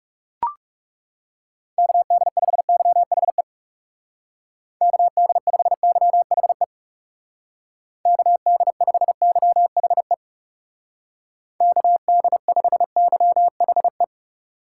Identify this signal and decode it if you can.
Morse code, heard as a keyed tone, about 36 wpm, 700 Hz. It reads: KD5YHE KD5YHE KD5YHE KD5YHE
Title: Speed Racing - USA Call Signs 30wpm